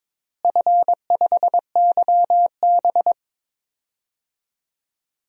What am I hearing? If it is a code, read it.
Morse code, heard as a keyed tone, about 22 wpm, 700 Hz. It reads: F5YB
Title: DXCC-100 Call Signs - 4 Characters - Repeated 3x - 22wpm